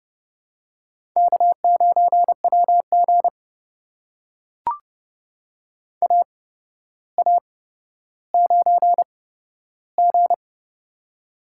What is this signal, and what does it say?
Morse code, heard as a keyed tone, about 30 wpm, spaced as for 13 wpm, 700 Hz. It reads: K9WG AA9G
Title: USA Call Signs - 4 Characters - 8x Character Spacing - 30wpm